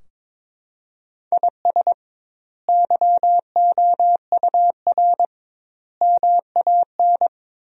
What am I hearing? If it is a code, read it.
Morse code, heard as a keyed tone, about 22 wpm, 700 Hz. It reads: IS YOUR MAN
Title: 2x Word-Spacing - Sentences from Top 400 Words - 22wpm